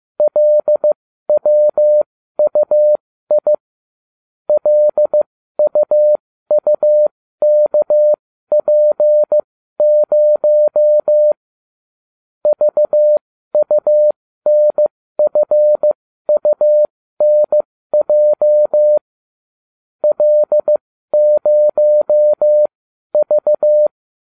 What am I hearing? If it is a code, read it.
Morse code, heard as a keyed tone, about 15 wpm, spaced as for 12 wpm, 600 Hz. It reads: LWUI LUUKP0 VUNFUNJ L0V
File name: cw / ubacw20 2